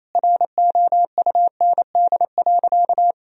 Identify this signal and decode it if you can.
Morse code, heard as a keyed tone, about 27 wpm, 700 Hz. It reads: ROUND.
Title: Sentences from Top 1000 Words - Review Part 1 - 28wpm